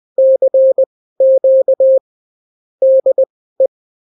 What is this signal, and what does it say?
Morse code, heard as a keyed tone, about 20 wpm, spaced as for 15 wpm, 530 Hz. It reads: CQ DE
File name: CW QSO libroCW4 20-10ppm